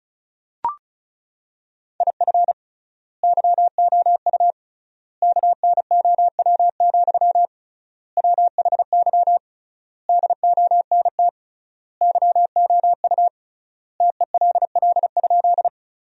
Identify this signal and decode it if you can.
Morse code, heard as a keyed tone, about 36 wpm, 700 Hz. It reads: IF YOU KNOW, WHY DONT YOU TELL?
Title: 3x Word-Spacing - Sentences from Top 300 Words - 35wpm